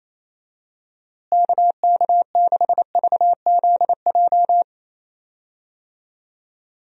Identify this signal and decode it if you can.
Morse code, heard as a keyed tone, about 28 wpm, 700 Hz. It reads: KK6VZJ
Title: USA Call Signs - 6 Characters - Repeated 2x - 28wpm